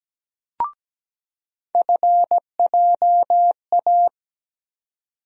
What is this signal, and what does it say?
Morse code, heard as a keyed tone, about 17 wpm, 700 Hz. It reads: FJA
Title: USA Call Sign Postfixes - 3 Letters 17wpm